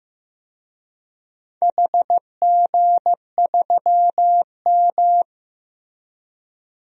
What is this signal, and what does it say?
Morse code, heard as a keyed tone, about 15 wpm, 700 Hz. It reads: HG3M
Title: DXCC-100 Call Signs - 4 Characters - Repeated 3x - 15wpm